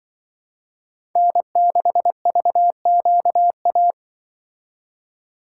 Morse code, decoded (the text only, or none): N6VQA